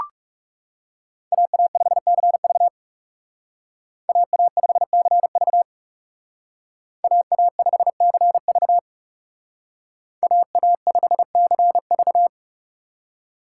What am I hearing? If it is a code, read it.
Morse code, heard as a keyed tone, about 35 wpm, 700 Hz. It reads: AA5CV AA5CV AA5CV AA5CV
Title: Speed Racing - USA 2x2 Call Signs 30wpm